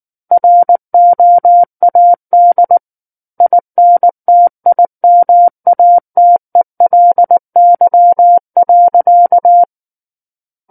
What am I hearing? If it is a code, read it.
Morse code, heard as a keyed tone, about 19 wpm, 700 Hz. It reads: ROAD INTIMATELY.
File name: Book One / 10-Chapter 10 - 19 WPM